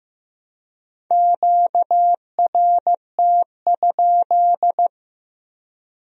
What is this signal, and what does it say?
Morse code, heard as a keyed tone, about 18 wpm, 700 Hz. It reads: QRT?
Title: Speed-Racing Q-Codes - 15wpm